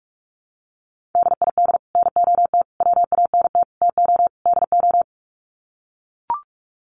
Morse code, heard as a keyed tone, about 44 wpm, 700 Hz. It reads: DID NOT WANT TO DO
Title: Sets of 5 Words Encore 45wpm